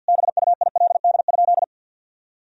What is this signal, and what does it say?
Morse code, heard as a keyed tone, about 50 wpm, 700 Hz. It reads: BUILD?
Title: Sentences from Top 1000 Words - Review Part 3 - 50wpm